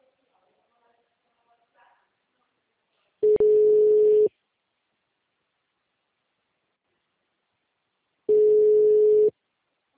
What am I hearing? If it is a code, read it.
Ringback tone (Europe, 425 Hz), with speech.